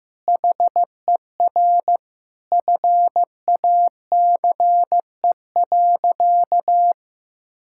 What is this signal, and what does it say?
Morse code, heard as a keyed tone, about 15 wpm, 700 Hz. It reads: HER FACE.
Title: Sentences from Top 1600 Words 15wpm